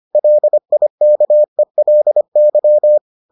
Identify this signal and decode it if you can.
Morse code, heard as a keyed tone, about 25 wpm, 600 Hz. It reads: LIKELY